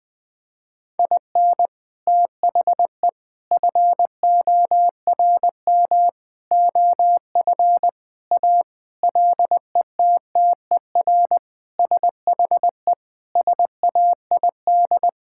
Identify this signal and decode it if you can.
Morse code, heard as a keyed tone, about 20 wpm, 700 Hz. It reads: IN THE FORM OF A LETTER SHE SAID
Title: Sentences with 4-Word Phrases - 20wpm